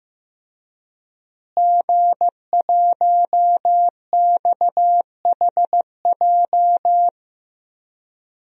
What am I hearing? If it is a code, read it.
Morse code, heard as a keyed tone, about 15 wpm, 700 Hz. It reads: G1XHJ
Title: DXCC-100 Call Signs - 5 Characters - Repeated 3x - 15wpm